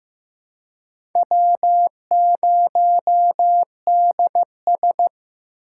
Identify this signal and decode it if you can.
Morse code, heard as a keyed tone, about 15 wpm, 700 Hz. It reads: W0DS
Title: USA Call Signs - 4 Characters - 6x Character Spacing - 15wpm